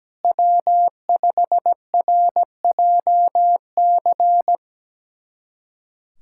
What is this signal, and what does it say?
Morse code, heard as a keyed tone, about 17 wpm, 700 Hz. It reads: W5RJC
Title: USA Callsign DE Callsign - 5 Characters - 17wpm